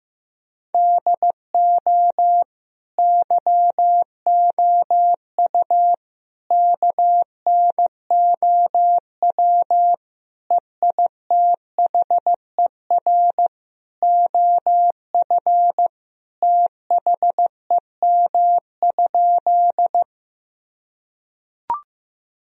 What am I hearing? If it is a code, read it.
Morse code, heard as a keyed tone, about 15 wpm, 700 Hz. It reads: DO YOU KNOW EITHER OF THEM?
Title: Sentences from Top 1000 Words - Review Part 4 - 15wpm